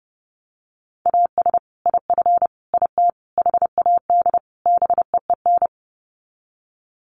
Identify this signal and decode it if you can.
Morse code, heard as a keyed tone, about 30 wpm, 700 Hz. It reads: AS IF IT HAD BEEN
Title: Sets of 5 Words Encore2 30wpm